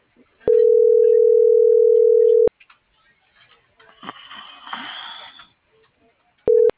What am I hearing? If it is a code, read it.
Ringback tone (North America), 440 + 480 Hz.